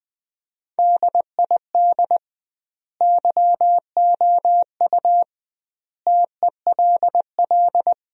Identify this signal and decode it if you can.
Morse code, heard as a keyed tone, about 20 wpm, 700 Hz. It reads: DID YOU TELL
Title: 2x Word-Spacing - Sentences from Top 400 Words - 20wpm